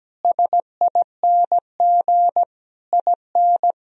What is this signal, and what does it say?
Morse code, heard as a keyed tone, about 17 wpm, 700 Hz. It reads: SING IN
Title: Sentences from Top 1000 Words - Review Part 2 - 17wpm